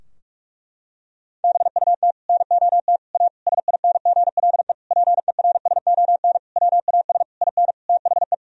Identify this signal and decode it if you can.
Morse code, heard as a keyed tone, about 46 wpm, 700 Hz. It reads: BUT NOT A SINGLE PERSON WAS IN THE
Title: Sentences from Top 1000 Words - Review Part 4 - 45wpm